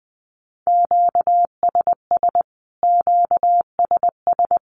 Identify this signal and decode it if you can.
Morse code, heard as a keyed tone, about 20 wpm, 700 Hz. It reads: QSS QSS
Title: Q-Codes Repeated 3x 20wpm